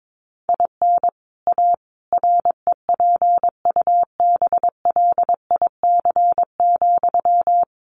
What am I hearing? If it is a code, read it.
Morse code, heard as a keyed tone, about 22 wpm, 700 Hz. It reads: IN A REPUBLIC,
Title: Sentences from Top 1500 Words 22wpm